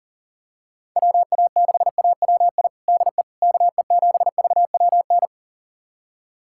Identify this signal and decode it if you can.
Morse code, heard as a keyed tone, about 40 wpm, 700 Hz. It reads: WA6AWI DE KE7VWN
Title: USA Callsign DE Callsign - 6 Characters - 40wpm